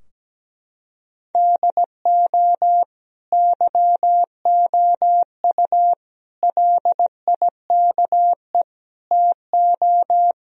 Morse code, heard as a keyed tone, about 17 wpm, 700 Hz. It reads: DO YOU LIKE TO
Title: Sentences from Top 1000 Words - Review Part 2 - 17wpm